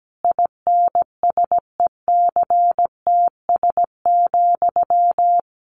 Morse code, heard as a keyed tone, about 17 wpm, 700 Hz. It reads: INSECTS,
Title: A Sound Of Thunder 17wpm